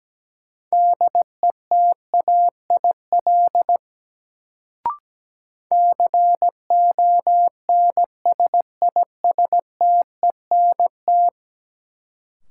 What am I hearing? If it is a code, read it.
Morse code, heard as a keyed tone, about 17 wpm, 700 Hz. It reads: DETAIL CONSISTENT
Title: Sentences from Top 1500 Words - New Word Drill - 17wpm